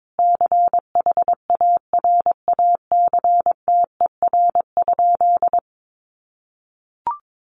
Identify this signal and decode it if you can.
Morse code, heard as a keyed tone, about 22 wpm, 700 Hz. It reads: CHARACTER?
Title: Sentences from Top 1200 Words 22wpm